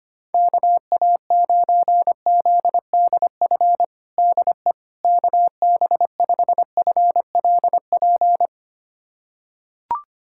Morse code, heard as a keyed tone, about 25 wpm, 700 Hz. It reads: KA9ZDF DE KB5FLP
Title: USA Callsign DE Callsign - 6 Characters - 25wpm